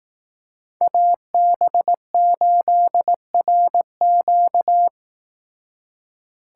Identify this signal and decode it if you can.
Morse code, heard as a keyed tone, about 18 wpm, 700 Hz. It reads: AB8RQ
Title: Speed Racing - USA Call Signs 15wpm